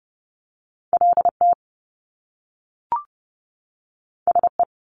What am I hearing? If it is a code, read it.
Morse code, heard as a keyed tone, about 30 wpm, 700 Hz. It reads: LT SE
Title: Top Two Letter Combinations in Words 30wpm